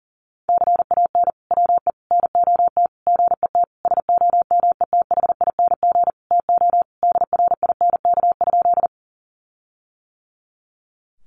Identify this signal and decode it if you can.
Morse code, heard as a keyed tone, about 40 wpm, 700 Hz. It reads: CAN WE NOT GET SOMETHING TO DRINK?
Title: Sentences from Top 1500 Words 40wpm